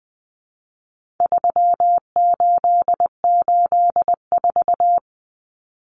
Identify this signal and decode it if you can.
Morse code, heard as a keyed tone, about 20 wpm, 700 Hz. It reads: 3884
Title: Common Cut Numbers - 4-Digit 20wpm